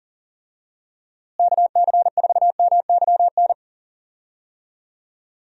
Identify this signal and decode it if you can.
Morse code, heard as a keyed tone, about 40 wpm, 700 Hz. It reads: KC4MYN
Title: USA Call Signs - 6 Characters - Repeated 2x - 40wpm